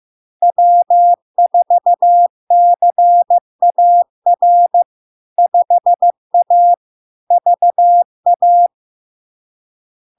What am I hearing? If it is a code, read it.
Morse code, heard as a keyed tone, about 15 wpm, 700 Hz. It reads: W4CAR 5A VA
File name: LARK-FD-2022-Log-in-CW-15wpm-700-Hz